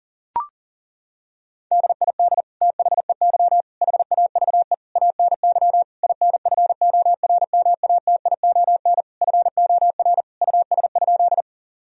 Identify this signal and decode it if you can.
Morse code, heard as a keyed tone, about 40 wpm, 700 Hz. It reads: DID THEY HAVE ANY INFORMATION FOR US?
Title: Sentences from Top 1000 Words - Review Part 3 - 40wpm